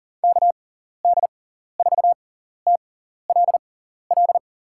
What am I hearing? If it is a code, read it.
Morse code, heard as a keyed tone, about 40 wpm, spaced as for 14 wpm, 700 Hz. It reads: KD4TLL
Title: USA Call Signs - 6 Characters - 6x Character Spacing - 40wpm